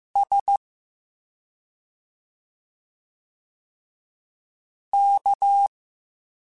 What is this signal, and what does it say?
Morse code, heard as a keyed tone, about 15 wpm, 800 Hz. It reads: S K